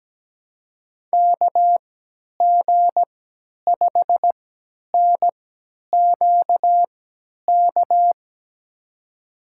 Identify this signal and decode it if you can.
Morse code, heard as a keyed tone, about 17 wpm, spaced as for 10 wpm, 700 Hz. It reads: KG5NQK
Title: USA Call Signs - 6 Characters - 3x Character Spacing - 17wpm